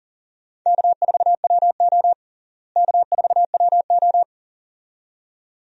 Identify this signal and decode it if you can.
Morse code, heard as a keyed tone, about 40 wpm, 700 Hz. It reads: K4WO K4WO
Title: USA Call Signs - 4 Characters - Repeated 2x - 40wpm